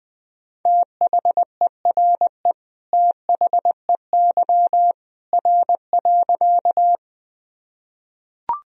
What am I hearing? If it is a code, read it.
Morse code, heard as a keyed tone, about 20 wpm, 700 Hz. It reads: THERE THEY R.
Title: Sentences from Top 100 Words and Abbreviations 20wpm